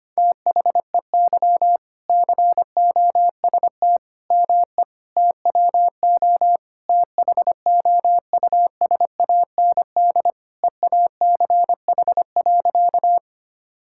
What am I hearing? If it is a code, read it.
Morse code, heard as a keyed tone, about 25 wpm, 700 Hz. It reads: THEY COST ME TWO THOUSAND EACH.
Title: Sentences from Top 1000 Words - Review Part 5 - 25wpm